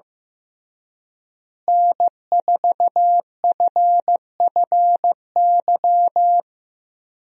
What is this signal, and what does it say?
Morse code, heard as a keyed tone, about 15 wpm, 700 Hz. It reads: N4FFY
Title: USA Call Signs - 5 Characters - Repeated 3x - 15wpm